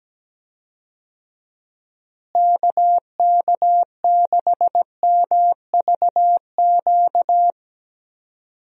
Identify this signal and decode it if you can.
Morse code, heard as a keyed tone, about 17 wpm, 700 Hz. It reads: KK6MVQ
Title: USA Call Signs - 6 Characters - Repeated 3x - 17wpm